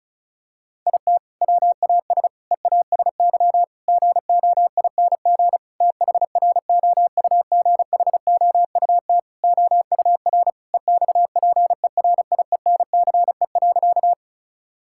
Morse code, heard as a keyed tone, about 34 wpm, 700 Hz. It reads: IT WAS EASY GOING THROUGHOUT OUR EXPERIENCE.
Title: Sentences from Top 1000 Words - Review Part 5 - 35wpm